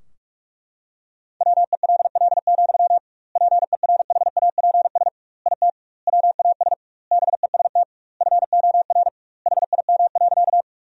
Morse code, heard as a keyed tone, about 41 wpm, 700 Hz. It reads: WELL, PERHAPS IT WAS BEST FOR HIM.
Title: 2x Word-Spacing - Sentences from Top 500 Words - 45wpm